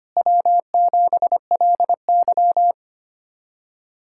Morse code, heard as a keyed tone, about 25 wpm, 700 Hz. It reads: W7LY
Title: USA Callsign DE Callsign - 4 Characters - 25wpm